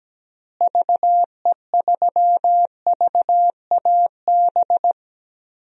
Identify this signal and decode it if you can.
Morse code, heard as a keyed tone, about 17 wpm, 700 Hz. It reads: VE3VAB
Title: DXCC-100 Call Signs - 6 Characters - Repeated 3x - 17wpm